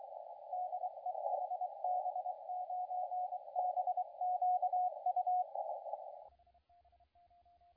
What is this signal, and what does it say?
Morse code, heard as a keyed tone, about 24 wpm, 730 Hz, with radio noise. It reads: DE HB9BQU
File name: la4c / sac-cw-2017 / CONTEST AUDIO 2100Z